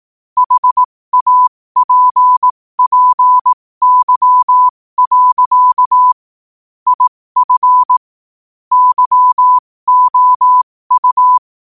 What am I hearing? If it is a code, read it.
Morse code, heard as a keyed tone, about 18 wpm, spaced as for 15 wpm, 1000 Hz. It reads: HAPPY. IF YOU
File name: qotdfiles / 15wpm